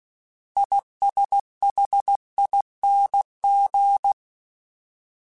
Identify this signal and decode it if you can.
Morse code, heard as a keyed tone, about 16 wpm, 800 Hz. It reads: ISHING
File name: Combinations / Combinations-mp3s / Combinations@15